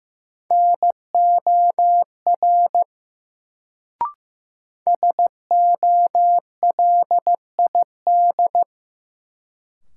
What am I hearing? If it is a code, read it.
Morse code, heard as a keyed tone, about 15 wpm, 700 Hz. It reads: NOR SOLID
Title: Sentences from Top 1500 Words - New Word Drill Alt - 15wpm